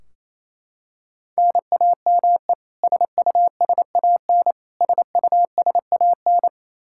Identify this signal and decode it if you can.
Morse code, heard as a keyed tone, about 28 wpm, 700 Hz. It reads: NAME SUSAN SUSAN
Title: Practice QSO - Exchange 1 - Beginning - 28wpm